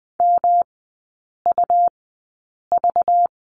Morse code, heard as a keyed tone, about 20 wpm, spaced as for 8 wpm, 700 Hz. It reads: MUV
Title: 191 Lesson 31e - Letter v - Three Characters 20wpm